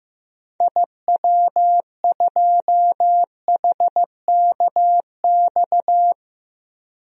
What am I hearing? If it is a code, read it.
Morse code, heard as a keyed tone, about 15 wpm, 700 Hz. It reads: IW2HKX
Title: DXCC-100 Call Signs - 6 Characters - Repeated 3x - 15wpm